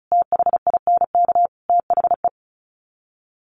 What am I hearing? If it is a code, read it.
Morse code, heard as a keyed tone, about 34 wpm, 700 Hz. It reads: THINK THE
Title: Sets of 5 Words Encore 35wpm